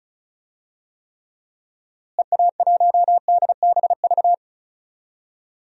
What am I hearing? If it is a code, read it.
Morse code, heard as a keyed tone, about 36 wpm, 700 Hz. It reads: EA1DBV
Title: DXCC-100 Call Signs - 6 Characters - Repeated 3x - 35wpm